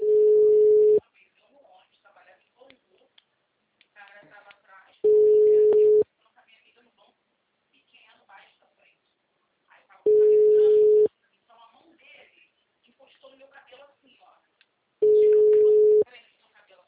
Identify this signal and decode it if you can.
Ringback tone (Europe, 425 Hz), with speech.
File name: out- -2001- - - .68584